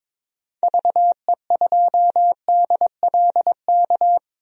Morse code, heard as a keyed tone, about 22 wpm, 700 Hz. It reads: VE2DLK
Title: DXCC-100 Call Signs - 6 Characters - 22wpm